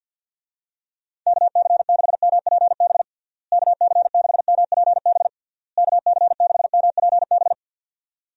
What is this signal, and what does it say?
Morse code, heard as a keyed tone, about 50 wpm, 700 Hz. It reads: KC6MPB KC6MPB KC6MPB
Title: USA Call Signs - 6 Characters - Repeated 3x - 50wpm